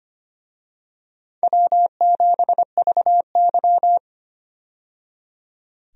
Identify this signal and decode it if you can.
Morse code, heard as a keyed tone, about 25 wpm, 700 Hz. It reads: W7VY
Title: USA Call Signs - 4 Characters - 25wpm